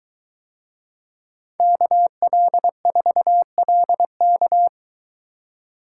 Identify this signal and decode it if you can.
Morse code, heard as a keyed tone, about 22 wpm, 700 Hz. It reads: KL4LK